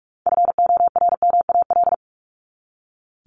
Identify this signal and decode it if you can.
Morse code, heard as a keyed tone, about 44 wpm, 700 Hz. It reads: FORMAL
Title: Top 1251-1500 Words 45wpm